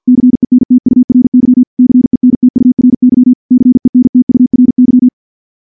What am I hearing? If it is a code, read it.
Morse code, heard as a keyed tone, about 49 wpm, 250 Hz. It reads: CENTURY CENTURY CENTURY
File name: morse / morse-training-50WPM